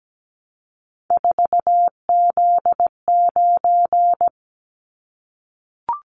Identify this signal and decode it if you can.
Morse code, heard as a keyed tone, about 17 wpm, 700 Hz. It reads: 4Z9
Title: Callsign Prefixes for Top 50 DXCC - 17wpm